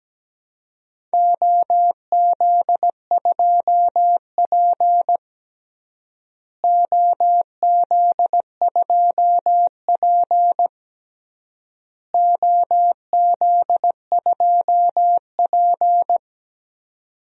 Morse code, heard as a keyed tone, about 17 wpm, 700 Hz. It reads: OZ2P OZ2P OZ2P
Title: DXCC-100 Call Signs - 4 Characters - Repeated 3x - 17wpm